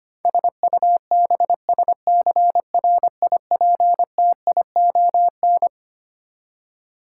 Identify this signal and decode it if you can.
Morse code, heard as a keyed tone, about 25 wpm, 700 Hz. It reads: SUBSCRIPTION
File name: Twelve Letter Words - 25wpm